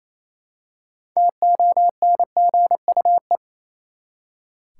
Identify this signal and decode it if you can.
Morse code, heard as a keyed tone, about 28 wpm, 700 Hz. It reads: TONGUE
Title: Six Letter Words - 28wpm